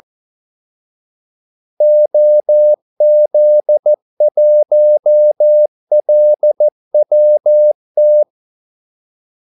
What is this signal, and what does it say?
Morse code, heard as a keyed tone, about 14 wpm, 600 Hz. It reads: OZ1LWT
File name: Callsigns-14wpm